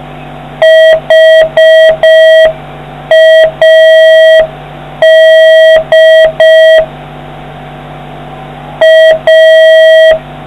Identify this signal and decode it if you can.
Morse code, heard as a keyed tone, about 5 wpm, 1950 Hz, with noise. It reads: HAD A